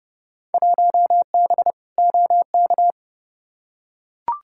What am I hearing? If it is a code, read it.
Morse code, heard as a keyed tone, about 32 wpm, 700 Hz. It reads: 1B OK
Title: Field Day Exchanges Encore - 30wpm